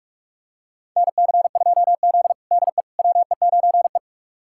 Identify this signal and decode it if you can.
Morse code, heard as a keyed tone, about 46 wpm, 700 Hz. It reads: NC2Z DE WE9E
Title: USA Callsign DE Callsign - 4 Characters - 45wpm